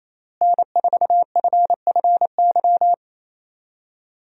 Morse code, heard as a keyed tone, about 28 wpm, 700 Hz. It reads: N4FFY
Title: USA Call Signs - 5 Characters - Repeated 3x - 28wpm